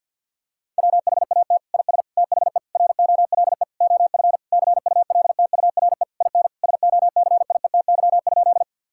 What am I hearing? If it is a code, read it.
Morse code, heard as a keyed tone, about 50 wpm, 700 Hz. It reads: WHAT IS THE ROLE OF CULTURE IN SOCIETY?
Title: Sentences from Top 1000 Words - Review Part 6 - 50wpm